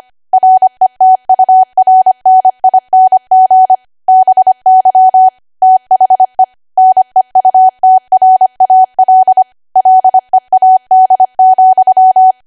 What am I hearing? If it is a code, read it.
Morse code, heard as a keyed tone, about 25 wpm, 750 Hz. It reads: RETURNING BY THE NEUTRAL LEAD,
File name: cwa / W1AW-Level3 / 25WPM